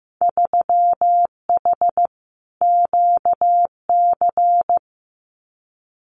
Morse code, heard as a keyed tone, about 15 wpm, 700 Hz. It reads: 3H QC
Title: Winter Field Day Exchanges Encore 15wpm